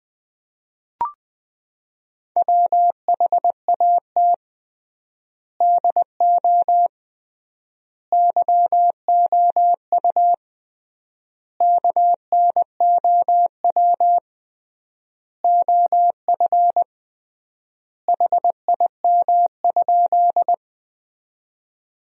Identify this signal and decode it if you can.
Morse code, heard as a keyed tone, about 20 wpm, 700 Hz. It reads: WHAT DO YOU KNOW OF HIM?
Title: 3x Word-Spacing - Sentences from Top 100 Words - 20wpm